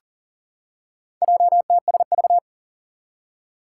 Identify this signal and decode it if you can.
Morse code, heard as a keyed tone, about 38 wpm, 700 Hz. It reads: JTSV
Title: Four Challenging Letter-Number Combinations 40wpm